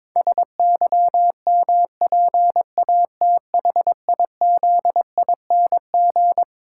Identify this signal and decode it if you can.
Morse code, heard as a keyed tone, about 22 wpm, 700 Hz. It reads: SYMPATHIZING
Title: Pangram Challenge v2 - 22wpm